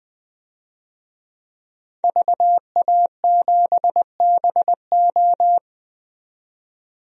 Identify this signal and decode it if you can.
Morse code, heard as a keyed tone, about 20 wpm, 700 Hz. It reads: VA7BO
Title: DXCC-100 Call Signs - 5 Characters - 20wpm